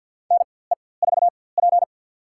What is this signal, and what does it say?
Morse code, heard as a keyed tone, about 53 wpm, spaced as for 24 wpm, 700 Hz. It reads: NE4P
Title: USA Call Signs - 4x Character Spacing - 50wpm